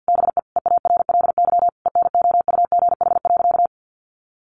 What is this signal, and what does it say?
Morse code, heard as a keyed tone, about 50 wpm, 700 Hz. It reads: BE EARLY ENOUGH.
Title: Sentences from Top 400 Words 50wpm